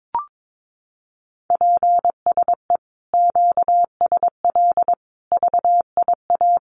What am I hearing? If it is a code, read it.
Morse code, heard as a keyed tone, about 22 wpm, 700 Hz. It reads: PSE QSL VIA
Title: Common QSO Phrases Encore 22wpm